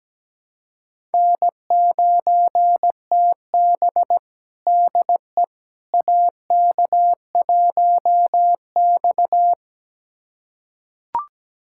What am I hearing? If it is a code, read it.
Morse code, heard as a keyed tone, about 17 wpm, 700 Hz. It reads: N9TB DE AK1X
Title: USA Callsign DE Callsign - 4 Characters - 17wpm